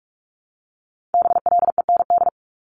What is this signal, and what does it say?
Morse code, heard as a keyed tone, about 46 wpm, 700 Hz. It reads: BLEND
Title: Multi-letter Phonemes Repeated 3x 45wpm